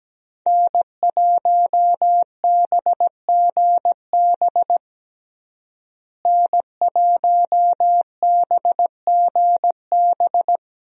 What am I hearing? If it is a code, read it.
Morse code, heard as a keyed tone, about 17 wpm, 700 Hz. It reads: N1BGB N1BGB
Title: USA Call Signs - 5 Characters - Repeated 2x - 17wpm